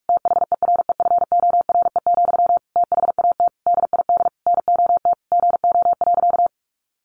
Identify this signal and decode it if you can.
Morse code, heard as a keyed tone, about 44 wpm, 700 Hz. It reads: THEREFORE, THAT DID NOT GO.
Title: Sentences from Top 800 Words 45wpm